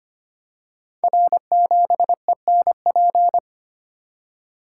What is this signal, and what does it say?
Morse code, heard as a keyed tone, about 25 wpm, 700 Hz. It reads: R7ENP
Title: Code Groups - 5 Letter:Numbers - 25wpm